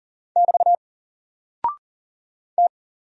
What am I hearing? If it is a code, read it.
Morse code, heard as a keyed tone, about 39 wpm, 700 Hz. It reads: <BT> T
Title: Single Character (Letter:Number:Punctuation:Prosign) 40wpm